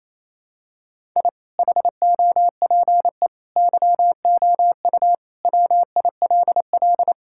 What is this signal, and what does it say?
Morse code, heard as a keyed tone, about 28 wpm, 700 Hz. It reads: I HOPE YOU WILL
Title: Sentences from Top 1000 Words - Review Part 5 - 28wpm